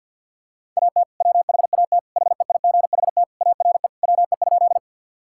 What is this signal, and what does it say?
Morse code, heard as a keyed tone, about 52 wpm, 700 Hz. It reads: AT WHAT HEIGHT ARE WE?
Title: Sentences from Top 1600 Words 50wpm